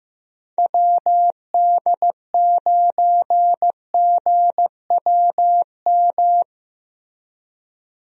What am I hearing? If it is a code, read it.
Morse code, heard as a keyed tone, about 15 wpm, 700 Hz. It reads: WD9GWM
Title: USA Call Signs - 6 Characters - Repeated 2x - 15wpm